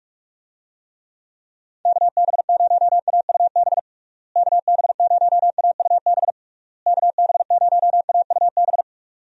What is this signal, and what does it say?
Morse code, heard as a keyed tone, about 44 wpm, 700 Hz. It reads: KB0AUB KB0AUB KB0AUB
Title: USA Call Signs - 6 Characters - Repeated 3x - 45wpm